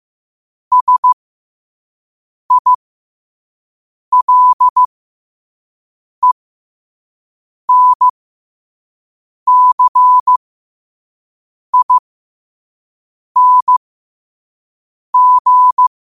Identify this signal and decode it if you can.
Morse code, heard as a keyed tone, about 15 wpm, spaced as for 5 wpm, 1000 Hz. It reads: SILENCING
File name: qotdfiles / 5wpm